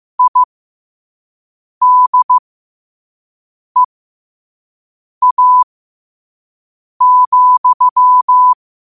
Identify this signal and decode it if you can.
Morse code, heard as a keyed tone, about 15 wpm, spaced as for 5 wpm, 1000 Hz. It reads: IDEA,
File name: qotdfiles / 5wpm